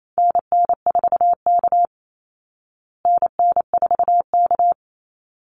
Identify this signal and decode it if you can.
Morse code, heard as a keyed tone, about 28 wpm, 700 Hz. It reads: NN4K NN4K
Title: Call Signs Repeated 3x - 28wpm